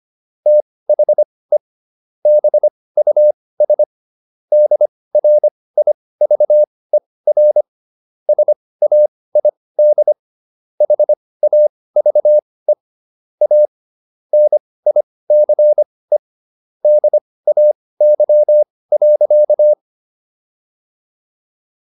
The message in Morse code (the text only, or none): THE BUS DRIVER SAID HAVE A NICE DAY.